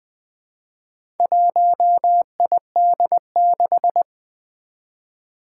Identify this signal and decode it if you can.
Morse code, heard as a keyed tone, about 20 wpm, 700 Hz. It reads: 1ID6
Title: Four Challenging Letter-Number Combinations 20wpm